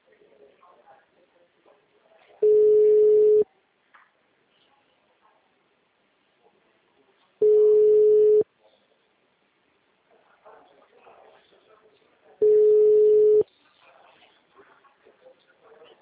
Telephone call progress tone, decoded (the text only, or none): ringback tone